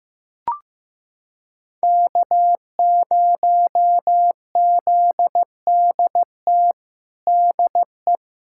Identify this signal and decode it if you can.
Morse code, heard as a keyed tone, about 15 wpm, 700 Hz. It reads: K0ZDT DE
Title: USA Callsign DE Callsign 15wpm